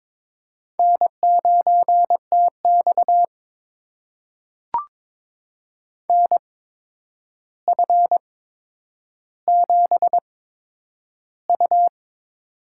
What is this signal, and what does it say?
Morse code, heard as a keyed tone, about 22 wpm, spaced as for 9 wpm, 700 Hz. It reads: N9TX NF7U
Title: USA Call Signs - 4 Characters - 8x Character Spacing - 22wpm